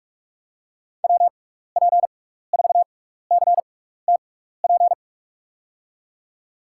Morse code, heard as a keyed tone, about 46 wpm, spaced as for 15 wpm, 700 Hz. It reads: WP4CTP
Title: USA Call Signs - 6 Characters - 6x Character Spacing - 45wpm